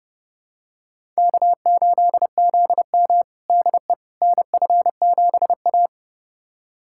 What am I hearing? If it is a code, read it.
Morse code, heard as a keyed tone, about 30 wpm, 700 Hz. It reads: K8ZM DE NF7A
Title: USA Callsign DE Callsign - 4 Characters - 30wpm